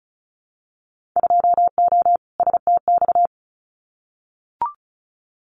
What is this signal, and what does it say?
Morse code, heard as a keyed tone, about 34 wpm, 700 Hz. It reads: 2O STX
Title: Winter Field Day Exchanges 35wpm